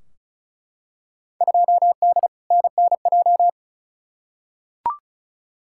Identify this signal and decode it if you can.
Morse code, heard as a keyed tone, about 36 wpm, 700 Hz. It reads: 2D NNJ
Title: Field Day Exchanges - 35wpm